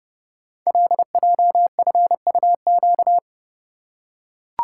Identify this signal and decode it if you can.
Morse code, heard as a keyed tone, about 30 wpm, 700 Hz. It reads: LJFUQ
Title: Code Groups - 5 Letters - 30wpm